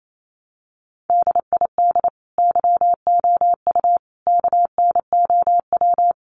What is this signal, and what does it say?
Morse code, heard as a keyed tone, about 28 wpm, 700 Hz. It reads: DID YOU KNOW
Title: Sentences from Top 100 Words 28wpm